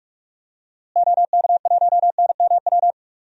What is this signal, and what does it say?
Morse code, heard as a keyed tone, about 46 wpm, 700 Hz. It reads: OK1NMW
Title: DXCC-100 Call Signs - 6 Characters - Repeated 3x - 45wpm